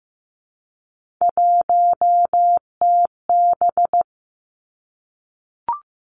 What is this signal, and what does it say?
Morse code, heard as a keyed tone, about 15 wpm, 700 Hz. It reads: 1TB
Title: Three Challenging Letter-Number Combinations 15wpm